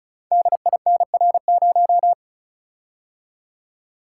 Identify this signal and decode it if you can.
Morse code, heard as a keyed tone, about 36 wpm, 690 Hz. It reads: DINR0
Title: Code Groups - 5 Letter:Numbers - 35wpm